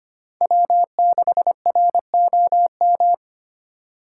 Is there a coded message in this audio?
Yes: Morse code from 0.4 to 3.1 s, heard as a keyed tone, about 33 wpm, 700 Hz. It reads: W6ROM